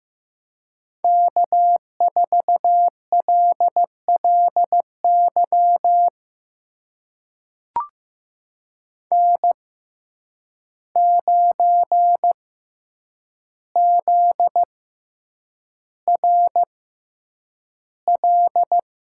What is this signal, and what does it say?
Morse code, heard as a keyed tone, about 15 wpm, spaced as for 8 wpm, 700 Hz. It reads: K4LLY N9ZRL
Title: USA Call Signs - 6x Character Spacing - 15wpm